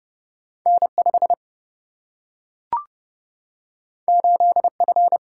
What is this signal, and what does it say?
Morse code, heard as a keyed tone, about 30 wpm, 700 Hz. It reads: N5 8F
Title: Single-Letter Single-Number 30wpm